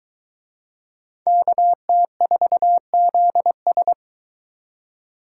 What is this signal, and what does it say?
Morse code, heard as a keyed tone, about 22 wpm, 700 Hz. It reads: KT4ZS